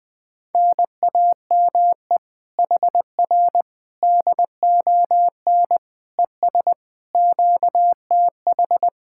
Morse code, heard as a keyed tone, about 20 wpm, 700 Hz. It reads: NAME HR DON ES QTH